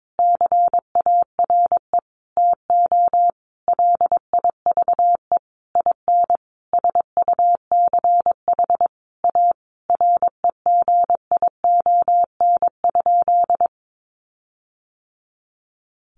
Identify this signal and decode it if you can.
Morse code, heard as a keyed tone, about 22 wpm, 700 Hz. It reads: CARE TO LIVE IN SUCH A REGION?
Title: Sentences from Top 1500 Words 22wpm